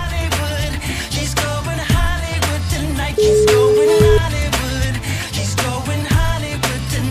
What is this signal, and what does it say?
Ringback tone (Europe, 425 Hz), with noise.